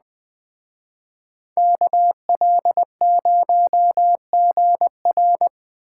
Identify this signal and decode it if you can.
Morse code, heard as a keyed tone, about 20 wpm, 700 Hz. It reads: KL0GR
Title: Speed Racing - USA 2x2 Call Signs 17wpm